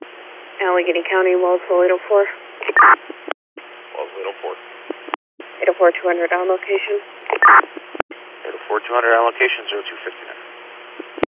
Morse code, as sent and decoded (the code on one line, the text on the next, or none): none